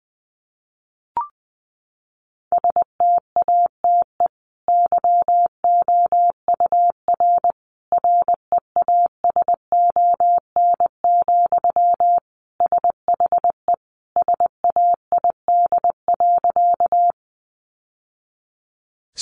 Morse code, heard as a keyed tone, about 20 wpm, 700 Hz. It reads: STATE YOUR REASON, SHE SAID.
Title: Sentences from Top 700 Words 20wpm